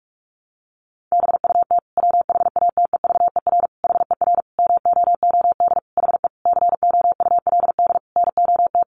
Morse code, heard as a keyed tone, about 46 wpm, 700 Hz. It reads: BUT WHATEVER HER MOOD HE COULD NOT
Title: Sentences from Top 1100 Words 45wpm